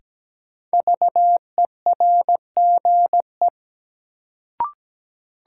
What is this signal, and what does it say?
Morse code, heard as a keyed tone, about 17 wpm, 700 Hz. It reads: VERGE
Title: Sets of 4 Words - New Word Drill - 17wpm